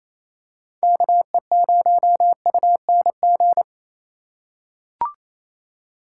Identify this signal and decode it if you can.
Morse code, heard as a keyed tone, about 28 wpm, 700 Hz. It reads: KE0UNG
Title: USA Call Signs - 4x Character Spacing - 28wpm